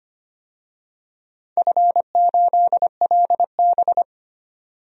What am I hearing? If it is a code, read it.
Morse code, heard as a keyed tone, about 25 wpm, 700 Hz. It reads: F8LB
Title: DXCC-100 Call Signs - 4 Characters - 25wpm